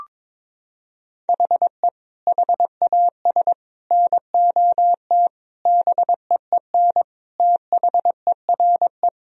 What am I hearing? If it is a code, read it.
Morse code, heard as a keyed tone, about 21 wpm, 700 Hz. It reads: HE HAS NOT BEEN THERE
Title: Sentences from Top 1000 Words - Review Part 1 - 22wpm